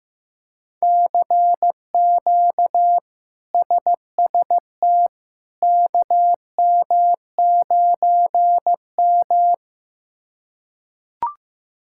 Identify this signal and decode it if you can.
Morse code, heard as a keyed tone, about 15 wpm, 700 Hz. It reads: CQ SST KM9M
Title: K1USN SST QSO Practice - 15wpm